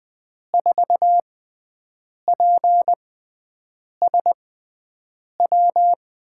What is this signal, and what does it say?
Morse code, heard as a keyed tone, about 20 wpm, spaced as for 7 wpm, 700 Hz. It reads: 4PSW
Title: USA Call Signs - 5 Characters - 6x Character Spacing - 20wpm